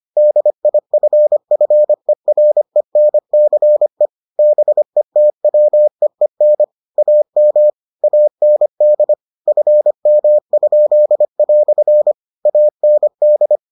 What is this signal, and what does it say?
Morse code, heard as a keyed tone, about 25 wpm, 600 Hz. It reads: DIFFERENCE BETWEEN AM AND FM?" AND